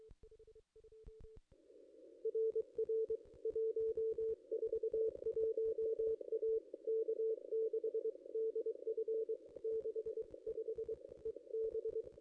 Morse code, heard as a keyed tone, about 23 wpm, 440 Hz. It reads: R R 141 A K6DF 65 EB
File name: contests / ss-cw-k5tr-op-n5rz -1900